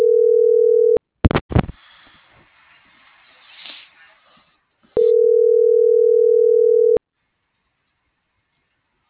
Ringback tone (North America), 440 + 480 Hz.